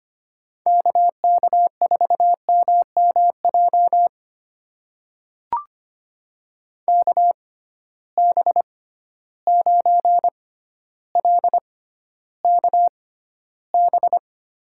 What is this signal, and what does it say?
Morse code, heard as a keyed tone, about 25 wpm, spaced as for 13 wpm, 700 Hz. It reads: KK4MMJ KB9LKB
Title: USA Call Signs - 6x Character Spacing - 25wpm